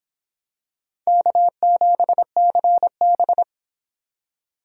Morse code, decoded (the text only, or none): K7CB